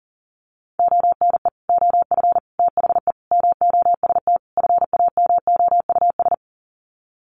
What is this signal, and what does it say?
Morse code, heard as a keyed tone, about 40 wpm, 700 Hz. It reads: ONE OF THE MOST FAMOUS
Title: Sets of 5 Words Encore2 40wpm